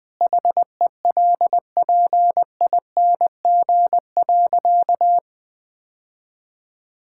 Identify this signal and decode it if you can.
Morse code, heard as a keyed tone, about 21 wpm, 700 Hz. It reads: HELPING.
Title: Sentences with 3-Word Phrases - 20wpm